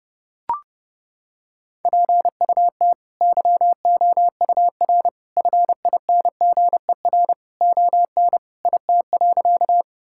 Morse code, heard as a keyed tone, about 30 wpm, 700 Hz. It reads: PUT YOUR FINGER ON IT.
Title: Sentences from Top 1600 Words 30wpm